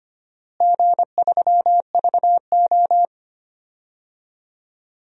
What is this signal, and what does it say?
Morse code, heard as a keyed tone, about 25 wpm, 700 Hz. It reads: G3VO
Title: DXCC-100 Call Signs - 4 Characters - Repeated 3x - 25wpm